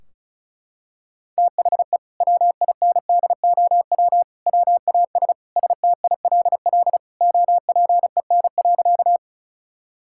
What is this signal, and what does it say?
Morse code, heard as a keyed tone, about 36 wpm, 700 Hz. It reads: THE WINDOW WAS STILL OPEN.
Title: Sentences from Top 1600 Words 35wpm